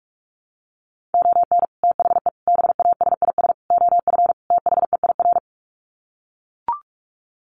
Morse code, heard as a keyed tone, about 44 wpm, 700 Hz. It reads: ON THE BASIS OF THEIR
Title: Sets of 5 Words Encore 45wpm